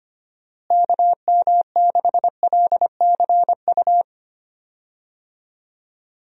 Morse code, heard as a keyed tone, about 25 wpm, 700 Hz. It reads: KM6LCU
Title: USA Call Signs - 6 Characters - Repeated 3x - 25wpm